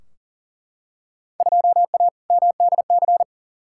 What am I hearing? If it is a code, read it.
Morse code, heard as a keyed tone, about 40 wpm, 700 Hz. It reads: 2A MDC
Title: Field Day Exchanges - 40wpm